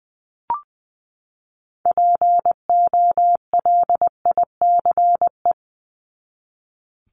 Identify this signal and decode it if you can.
Morse code, heard as a keyed tone, about 20 wpm, 700 Hz. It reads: POLICE
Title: Top 1751-2000 Words 20wpm